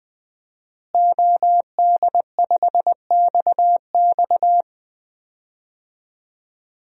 Morse code, heard as a keyed tone, about 21 wpm, 700 Hz. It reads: OD5XX
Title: DXCC-100 Call Signs - 5 Characters - Repeated 2x - 20wpm